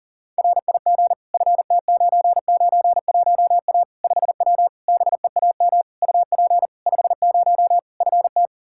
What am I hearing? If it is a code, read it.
Morse code, heard as a keyed tone, about 40 wpm, 700 Hz. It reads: RIG FT991A 5W BEAM UP 50 FT
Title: Practice QSO - Exchange 2 - Info - 40wpm